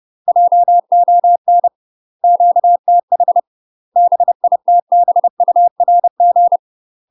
Morse code, heard as a keyed tone, about 30 wpm, 700 Hz. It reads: JON QTH BITBURG
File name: UK MASTER QSOs TRAINING / 30WPM / UK MASTER QSOs 0048